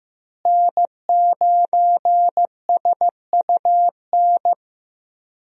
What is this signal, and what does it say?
Morse code, heard as a keyed tone, about 15 wpm, 700 Hz. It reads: N9SUN
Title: USA Call Signs - 5 Characters - Repeated 3x - 15wpm